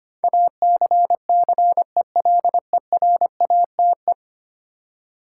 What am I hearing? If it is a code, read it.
Morse code, heard as a keyed tone, about 25 wpm, 700 Hz. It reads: ACCELERATE
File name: Ten Letter Words - 25wpm